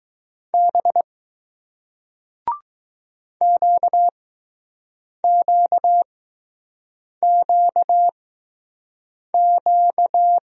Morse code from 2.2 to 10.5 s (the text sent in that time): QQQQ